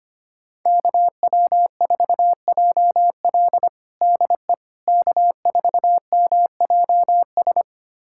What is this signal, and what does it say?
Morse code, heard as a keyed tone, about 25 wpm, 700 Hz. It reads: KW4JL DE K4MJS
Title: USA Callsign DE Callsign - 5 Characters - 25wpm